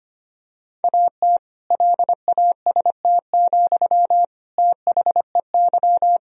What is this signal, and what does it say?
Morse code, heard as a keyed tone, about 25 wpm, 700 Hz. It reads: AT LAST, THEY
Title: Sentences from Top 1000 Words - Review Part 1 - 25wpm